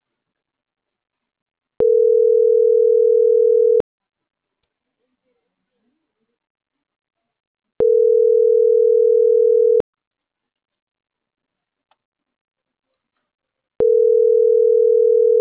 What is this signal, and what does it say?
Ringback tone (North America), 440 + 480 Hz.